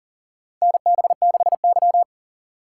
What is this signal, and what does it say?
Morse code, heard as a keyed tone, about 40 wpm, 700 Hz. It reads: NB6Y
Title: USA Call Signs - 4 Characters - 4x Character Spacing - 40wpm